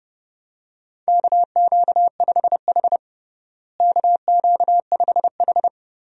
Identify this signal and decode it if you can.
Morse code, heard as a keyed tone, about 32 wpm, 700 Hz. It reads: KQ5H KQ5H
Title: USA Call Signs - 4 Characters - Repeated 3x - 30wpm